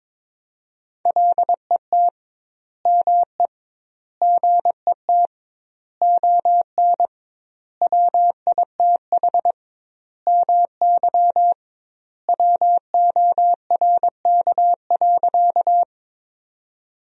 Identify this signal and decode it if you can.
Morse code, heard as a keyed tone, about 22 wpm, 700 Hz. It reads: LET ME GET ON WITH MY WORK.
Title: 2x Word-Spacing - Sentences from Top 300 Words - 22wpm